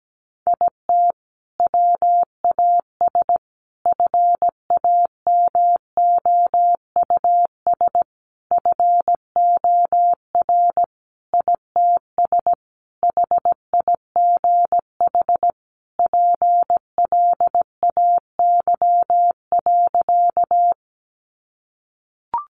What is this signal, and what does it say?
Morse code, heard as a keyed tone, about 17 wpm, 700 Hz. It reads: IT WAS FAMOUS FOR ITS HIGH PLAY.
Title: Sentences from Top 1500 Words 17wpm